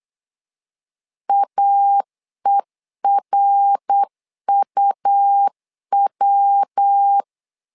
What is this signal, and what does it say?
Morse code, heard as a keyed tone, about 8 wpm, 800 Hz. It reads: AERUW
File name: LEZIONI / 3 MWRO / 3 MWR 8wpm 2444